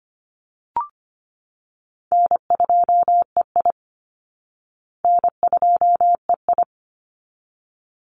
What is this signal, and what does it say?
Morse code, heard as a keyed tone, about 25 wpm, 700 Hz. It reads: N2EI N2EI
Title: Easy Call Signs Repeated 2x - 25wpm